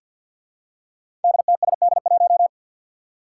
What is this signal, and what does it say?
Morse code, heard as a keyed tone, about 52 wpm, 700 Hz. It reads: DTSD1
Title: Five Challenging Letter-Number Combinations 50wpm